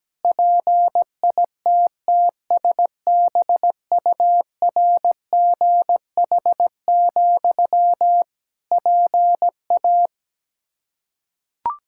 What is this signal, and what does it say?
Morse code, heard as a keyed tone, about 17 wpm, 700 Hz. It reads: PITTSBURGH, PA
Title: Practice QSO - Exchange 1 - Beginning - 17wpm